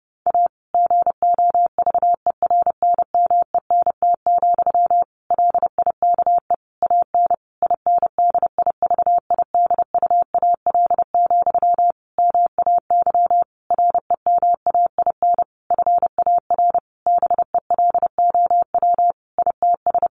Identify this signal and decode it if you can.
Morse code, heard as a keyed tone, about 30 wpm, 700 Hz. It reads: A GOVERNMENT, LIKE AN INDIVIDUAL, MAY REMAIN FAR BELOW ITS
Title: Sentences from Top 1400 Words 30wpm